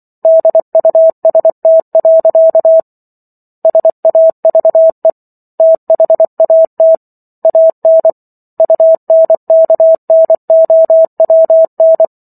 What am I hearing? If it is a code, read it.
Morse code, heard as a keyed tone, about 24 wpm, 650 Hz. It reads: DUST. SAVE THAT AN UNKNOWN